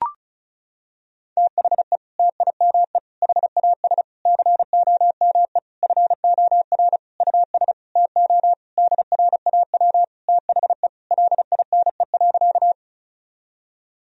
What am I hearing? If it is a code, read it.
Morse code, heard as a keyed tone, about 36 wpm, 700 Hz. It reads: THE TIME HAS COME FOR US TO DRAW THE LINE.
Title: Sentences from Top 1000 Words - Review Part 1 - 35wpm